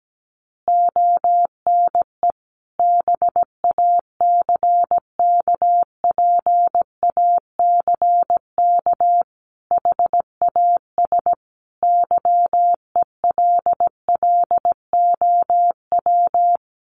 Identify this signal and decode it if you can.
Morse code, heard as a keyed tone, about 17 wpm, 700 Hz. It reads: ONE BACKPACK HAS YELLOW
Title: Additional Easy Phrases 17wpm